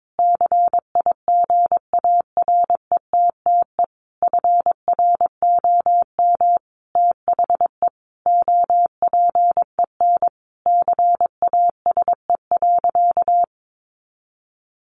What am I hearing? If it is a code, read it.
Morse code, heard as a keyed tone, about 22 wpm, 700 Hz. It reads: CIGARETTE FROM THE OPEN CASE.
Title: Sentences from Top 1000 Words 22wpm